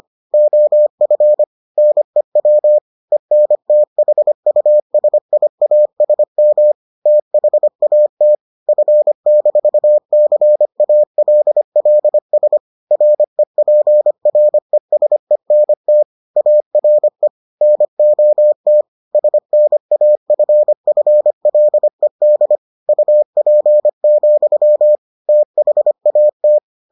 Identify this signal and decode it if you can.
Morse code, heard as a keyed tone, about 25 wpm, 600 Hz. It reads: OF NEW ENTHUSIASM THAT F-CALLS REPRESENT ARE NOT SNAFFLED UP, THAT